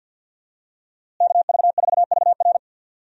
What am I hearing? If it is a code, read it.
Morse code, heard as a keyed tone, about 51 wpm, 700 Hz. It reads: KV4VR
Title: USA Call Signs - 5 Characters - Repeated 3x - 50wpm